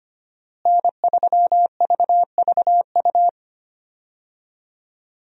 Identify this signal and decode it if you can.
Morse code, heard as a keyed tone, about 25 wpm, 700 Hz. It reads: N3VVU
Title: USA Call Signs - 5 Characters - Repeated 3x - 25wpm